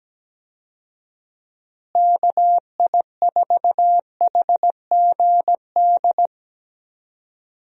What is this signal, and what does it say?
Morse code, heard as a keyed tone, about 17 wpm, 700 Hz. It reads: KI4HGD
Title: USA Call Signs - 6 Characters - Repeated 2x - 17wpm